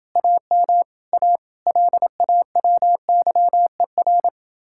Morse code, heard as a keyed tone, about 31 wpm, 700 Hz. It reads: AM A LAWYER